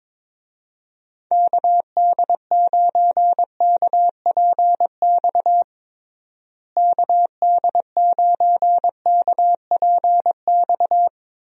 Morse code, heard as a keyed tone, about 22 wpm, 700 Hz. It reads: KD9KPX KD9KPX
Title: USA Call Signs - 6 Characters - Repeated 3x - 22wpm